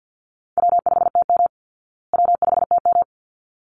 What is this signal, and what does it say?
Morse code, heard as a keyed tone, about 50 wpm, 700 Hz. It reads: W5TM W5TM
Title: Easy Call Signs Repeated 2x - 50wpm